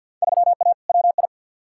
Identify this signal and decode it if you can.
Morse code, heard as a keyed tone, about 48 wpm, 700 Hz. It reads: 3A WI
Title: Field Day Exchanges - 50wpm